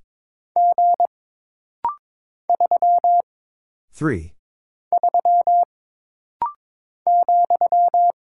Morse code, heard as a keyed tone, about 23 wpm, 700 Hz. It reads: G 3 3 ,
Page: This rapid-fire timing is geared to push your ICR (Instant Character Recognition) skills.